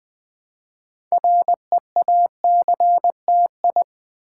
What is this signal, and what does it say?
Morse code, heard as a keyed tone, about 20 wpm, 700 Hz. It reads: REACTI
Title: Build A Word - 2 Letters At A Time - 8-Letter Words - 20wpm